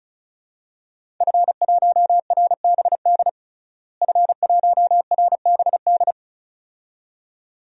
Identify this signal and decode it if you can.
Morse code, heard as a keyed tone, about 34 wpm, 700 Hz. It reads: F1RBD F1RBD
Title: DXCC-100 Call Signs - 5 Characters - Repeated 2x - 35wpm